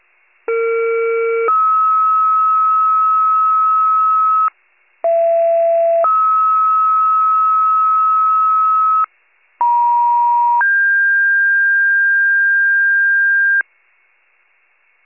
Two-tone page, heard as a one-second tone, then a three-second tone, about 460 then 1290 Hz, about 670 then 1290 Hz, about 950 then 1600 Hz, with noise. It is heard from 0.5 to 13.6 s.
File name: pager / 21 RESCUE / wcalert